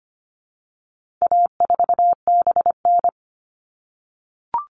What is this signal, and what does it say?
Morse code, heard as a keyed tone, about 25 wpm, 700 Hz. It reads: A4BN
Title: All Cut Numbers - 4-Digit 25wpm